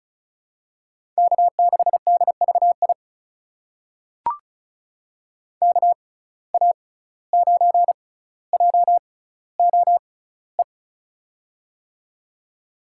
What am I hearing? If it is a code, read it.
Morse code, heard as a keyed tone, about 36 wpm, spaced as for 17 wpm, 700 Hz. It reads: K6DVI KA9JOE
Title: USA Call Signs - 6x Character Spacing - 35wpm